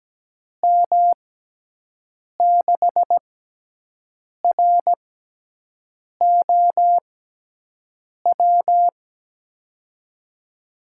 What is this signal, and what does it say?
Morse code, heard as a keyed tone, about 17 wpm, spaced as for 6 wpm, 700 Hz. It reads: M6ROW
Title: USA Call Signs - 6 Characters - 6x Character Spacing - 17wpm